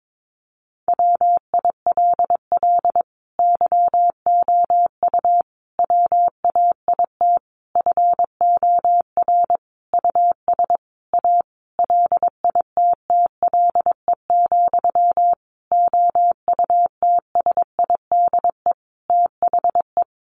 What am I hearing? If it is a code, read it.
Morse code, heard as a keyed tone, about 22 wpm, 700 Hz. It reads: WILL YOU WAIT FOR US A LITTLE, OUTSIDE THE
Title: Sentences from Top 900 Words 22wpm